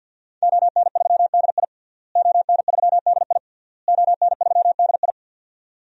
Morse code, heard as a keyed tone, about 52 wpm, 700 Hz. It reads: ON3DI ON3DI ON3DI
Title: DXCC-100 Call Signs - 5 Characters - Repeated 3x - 50wpm